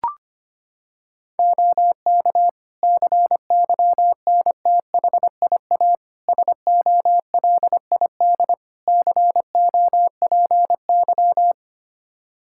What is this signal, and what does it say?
Morse code, heard as a keyed tone, about 25 wpm, 700 Hz. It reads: OK CYNTHIA SOLID COPY
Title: Practice QSO - Exchange 3 - More Info - 25wpm